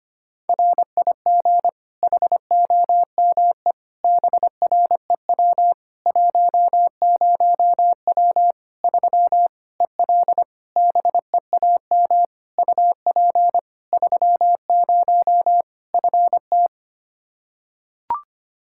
Morse code, heard as a keyed tone, about 25 wpm, 700 Hz. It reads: RIG HOME BREW 10W 3 EL BEAM UP 30 FT
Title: Practice QSO - Exchange 2 - Info - 25wpm